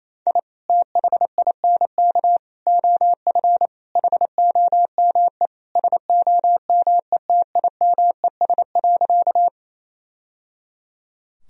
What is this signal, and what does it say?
Morse code, heard as a keyed tone, about 28 wpm, 700 Hz. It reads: I THINK OF HOME SOMETIMES.
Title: Sentences from Top 1000 Words - Review Part 3 - 28wpm